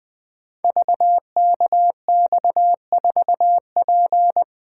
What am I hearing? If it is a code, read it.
Morse code, heard as a keyed tone, about 20 wpm, 700 Hz. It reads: VKX4P
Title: Code Groups - 5 Letter:Numbers - 20wpm